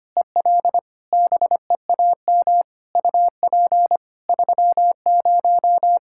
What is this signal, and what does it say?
Morse code, heard as a keyed tone, about 25 wpm, 700 Hz. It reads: EL BEAM UP 30
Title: Practice QSO - Exchange 2 - Info - 25wpm